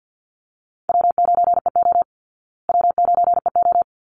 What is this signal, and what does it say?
Morse code, heard as a keyed tone, about 48 wpm, 700 Hz. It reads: W9EO W9EO
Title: Easy Call Signs Repeated 2x - 50wpm